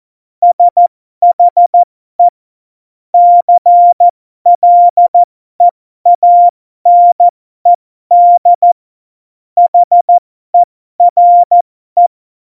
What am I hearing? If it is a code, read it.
Morse code, heard as a keyed tone, about 14 wpm, spaced as for 12 wpm, 700 Hz. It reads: SHE CLEANED HERE
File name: Morsekurs / Witze